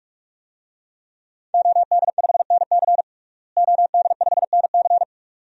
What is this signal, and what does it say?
Morse code, heard as a keyed tone, about 44 wpm, 700 Hz. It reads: OD5NC OD5NC
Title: DXCC-100 Call Signs - 5 Characters - Repeated 2x - 45wpm